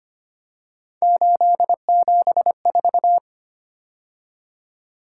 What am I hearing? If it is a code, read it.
Morse code, heard as a keyed tone, about 25 wpm, 700 Hz. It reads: 874
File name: Three Letter-Number Combinations - 25wpm